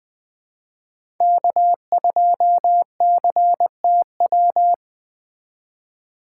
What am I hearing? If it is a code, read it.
Morse code, heard as a keyed tone, about 20 wpm, 700 Hz. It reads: K2CTW
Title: Speed Racing - USA 1x3 Call Signs 17wpm